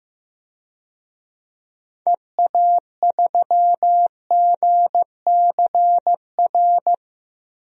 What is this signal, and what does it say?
Morse code, heard as a keyed tone, about 14 wpm, 700 Hz. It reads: EA3GCR
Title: DXCC-100 Call Signs - 6 Characters - Repeated 3x - 15wpm